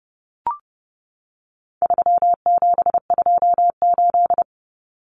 Morse code, heard as a keyed tone, about 28 wpm, 700 Hz. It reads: 3728
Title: Common Cut Numbers - 4-Digit 30wpm